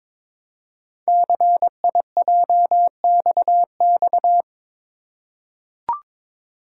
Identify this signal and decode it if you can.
Morse code, heard as a keyed tone, about 23 wpm, 700 Hz. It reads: CIJXX
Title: Code Groups - 5 Letters - 22wpm